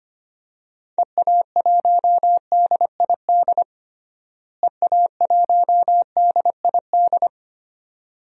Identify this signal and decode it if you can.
Morse code, heard as a keyed tone, about 25 wpm, 700 Hz. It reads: EA1DID EA1DID
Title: DXCC-100 Call Signs - 6 Characters - Repeated 2x - 25wpm